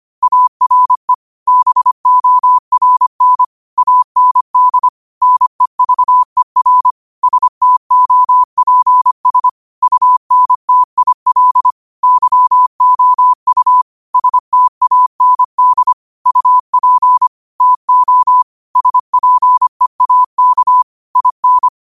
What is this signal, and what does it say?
Morse code, heard as a keyed tone, about 25 wpm, 1000 Hz. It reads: ARE BORN AND NEVER STOPS UNTIL YOU STAND UP TO SPEAK IN
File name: qotdfiles / 25wpm